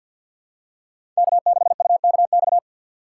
Morse code, heard as a keyed tone, about 52 wpm, 700 Hz. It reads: K6UKX
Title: USA Call Signs - 2x Character Spacing - 50wpm